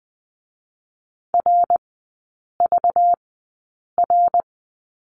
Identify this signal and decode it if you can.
Morse code, heard as a keyed tone, about 20 wpm, spaced as for 8 wpm, 700 Hz. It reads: RVR
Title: 191 Lesson 31e - Letter v - Three Characters 20wpm